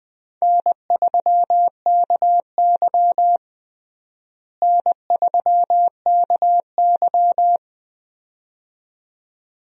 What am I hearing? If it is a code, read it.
Morse code, heard as a keyed tone, about 20 wpm, 700 Hz. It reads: N3KY N3KY
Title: USA Call Signs - 4 Characters - Repeated 2x - 20wpm